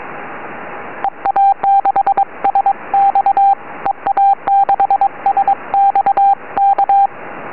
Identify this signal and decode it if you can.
Morse code, heard as a keyed tone, about 22 wpm, 750 Hz, with radio noise. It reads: EA6SX EA6SX K
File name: ea6sx qso 4m